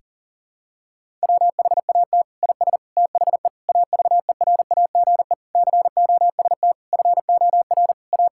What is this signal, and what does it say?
Morse code, heard as a keyed tone, about 40 wpm, 700 Hz. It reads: WHAT IS THE AVERAGE COST FOR A
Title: Sentences from Top 1000 Words - Review Part 6 - 40wpm